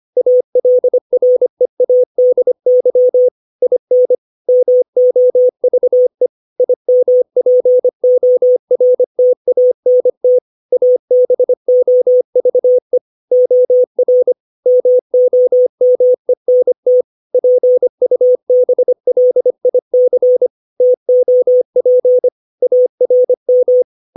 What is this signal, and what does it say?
Morse code, heard as a keyed tone, about 25 wpm, 500 Hz. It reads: ALREADY IN MOVE IMPORTANT ABOVE OR MOMENT PUBLIC TOP ARM
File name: 632 words / 632 Words 25WPM